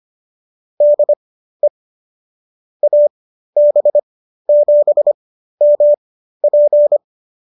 Morse code, heard as a keyed tone, about 25 wpm, spaced as for 13 wpm, 590 Hz. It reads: DE AB7MP